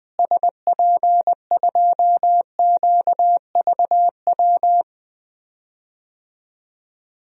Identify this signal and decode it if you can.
Morse code, heard as a keyed tone, about 20 wpm, 700 Hz. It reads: SP2QVW
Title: DXCC-100 Call Signs - 6 Characters - Repeated 3x - 20wpm